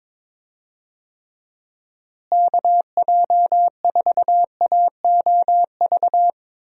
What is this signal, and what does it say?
Morse code, heard as a keyed tone, about 22 wpm, 700 Hz. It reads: KJ4AOV
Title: USA Call Signs - 6 Characters - Repeated 2x - 22wpm